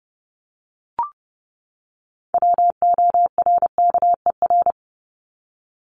Morse code, heard as a keyed tone, about 30 wpm, 700 Hz. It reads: WORKER
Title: ES:S Suffix - Top 100 Words with Base Words 30wpm